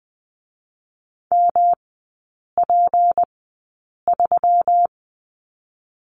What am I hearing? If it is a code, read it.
Morse code, heard as a keyed tone, about 20 wpm, spaced as for 8 wpm, 700 Hz. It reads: MP3
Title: 152 Lesson 25e - Letter p - Three Characters 20wpm